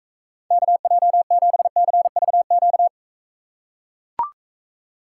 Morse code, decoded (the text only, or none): KJ7CVQ